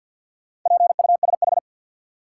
Morse code, heard as a keyed tone, about 48 wpm, 700 Hz. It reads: PUSH
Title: Sentences from Top 1300 Words - New Word Drill - 50wpm